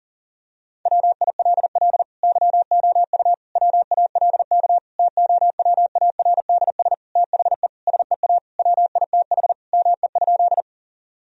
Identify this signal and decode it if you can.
Morse code, heard as a keyed tone, about 40 wpm, 700 Hz. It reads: WILL YOU WALK TOWARDS THE SEA WITH ME?
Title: Sentences from Top 1000 Words - Review Part 2 - 40wpm